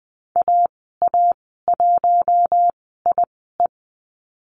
Morse code, heard as a keyed tone, about 20 wpm, spaced as for 14 wpm, 700 Hz. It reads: AA1IE
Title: 076 Lesson 13g - Letter L - Call Signs 20wpm